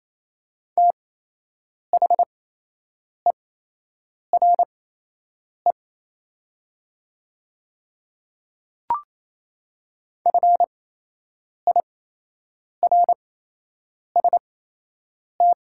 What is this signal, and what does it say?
Morse code, heard as a keyed tone, about 28 wpm, spaced as for 8 wpm, 700 Hz. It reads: THERE FIRST
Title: 8x Character Spacing - Top 100 Words - 28wpm